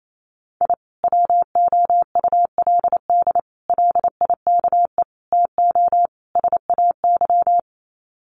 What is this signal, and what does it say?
Morse code, heard as a keyed tone, about 28 wpm, 700 Hz. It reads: I WOULD LIKE TO SAY
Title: Sets of 5 Words Encore2 28wpm